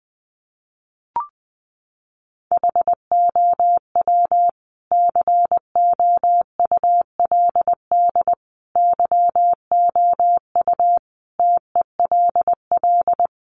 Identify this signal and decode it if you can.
Morse code, heard as a keyed tone, about 20 wpm, 700 Hz. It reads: HOW COULD YOU TELL
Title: Sentences from Top 1000 Words 20wpm